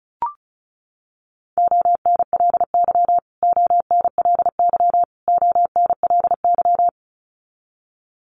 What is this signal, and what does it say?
Morse code, heard as a keyed tone, about 34 wpm, 700 Hz. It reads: ONLY ONLY ONLY
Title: Top 300 Words Repeated 35wpm